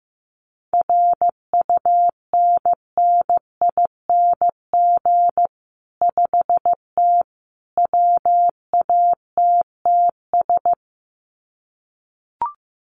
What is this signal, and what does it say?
Morse code, heard as a keyed tone, about 15 wpm, 700 Hz. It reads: RUNNING 5T WATTS
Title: Common QSO Phrases 15wpm